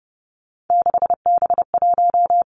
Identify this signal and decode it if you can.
Morse code, heard as a keyed tone, about 30 wpm, 700 Hz. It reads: 6B1
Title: Three Challenging Letter-Number Combinations 30wpm